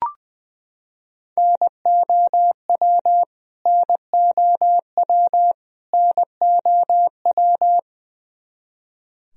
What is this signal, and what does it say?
Morse code, heard as a keyed tone, about 20 wpm, 700 Hz. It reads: NOW NOW NOW
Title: Top 100 Words Repeated 20wpm